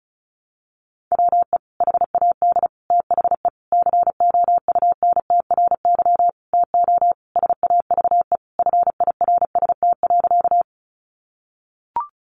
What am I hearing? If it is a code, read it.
Morse code, heard as a keyed tone, about 34 wpm, 700 Hz. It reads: WE HAD THE COUNTRY TO SAVE FIRST.
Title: Sentences from Top 1100 Words 35wpm